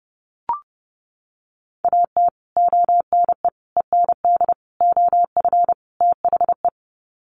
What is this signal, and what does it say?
Morse code, heard as a keyed tone, about 30 wpm, 700 Hz. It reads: AT ONE END OF THE
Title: Sets of 5 Words Encore 30wpm